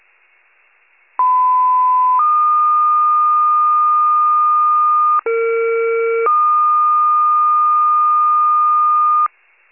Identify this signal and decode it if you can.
Two-tone page, heard as a one-second tone, then a three-second tone, about 1010 then 1220 Hz, about 460 then 1220 Hz. It is heard from 1.2 to 9.3 s.